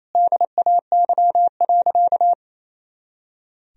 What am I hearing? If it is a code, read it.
Morse code, heard as a keyed tone, about 28 wpm, 700 Hz. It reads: DAY.
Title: Sentences from Top 300 Words and Abbreviations 28wpm